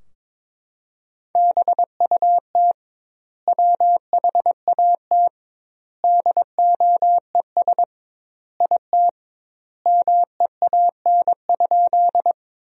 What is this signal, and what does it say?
Morse code, heard as a keyed tone, about 22 wpm, 700 Hz. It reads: BUT WHAT DOES IT MEAN?
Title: 2x Word-Spacing - Sentences from Top 200 Words - 22wpm